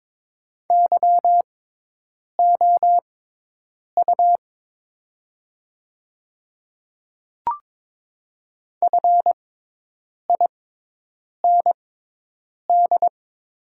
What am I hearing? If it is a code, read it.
Morse code, heard as a keyed tone, about 22 wpm, spaced as for 8 wpm, 700 Hz. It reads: YOU FIND
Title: 6x Character Spacing - Top 200 Words - 22wpm